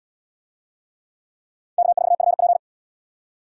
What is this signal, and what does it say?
Morse code, heard as a keyed tone, about 73 wpm, 700 Hz. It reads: K4FP